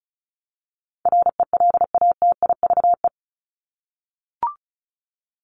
Morse code, heard as a keyed tone, about 34 wpm, 700 Hz. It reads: RELATIVE
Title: Top 1251-1500 Words 35wpm